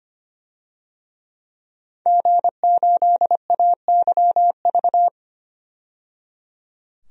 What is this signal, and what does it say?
Morse code, heard as a keyed tone, about 25 wpm, 700 Hz. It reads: G8AYV
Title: DXCC-100 Call Signs - 5 Characters - 25wpm